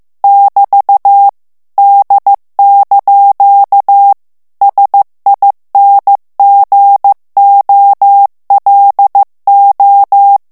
Morse code, heard as a keyed tone, about 15 wpm, 800 Hz. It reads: <BT> D) SINGOLO
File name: morse mp3 / ITALIEN texte clair 2 vitesse 14 mots minute